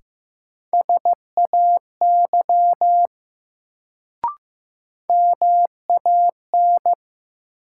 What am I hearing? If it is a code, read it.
Morse code, heard as a keyed tone, about 15 wpm, 700 Hz. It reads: SAY MAN
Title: Sentences from Top 200 Words - New Word Drill - 15wpm